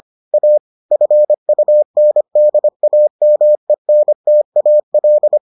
Morse code, heard as a keyed tone, about 25 wpm, 600 Hz. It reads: A FUNDAMENTAL